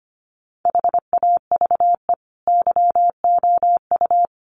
Morse code, heard as a keyed tone, about 25 wpm, 700 Hz. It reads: HAVE YOU
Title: Sentences from Top 1400 Words 25wpm